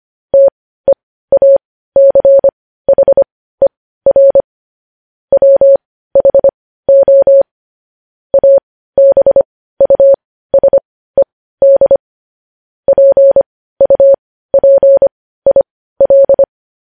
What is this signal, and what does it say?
Morse code, heard as a keyed tone, about 25 wpm, spaced as for 15 wpm, 550 Hz. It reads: TEACHER WHO ABUSED PUPIL
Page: news headlines in morse code at 15 wpm